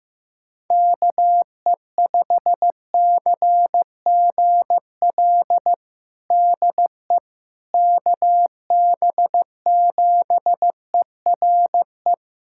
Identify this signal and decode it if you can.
Morse code, heard as a keyed tone, about 15 wpm, 700 Hz. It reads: KE5CGL DE KB7ERE
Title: USA Callsign DE Callsign - 6 Characters - 15wpm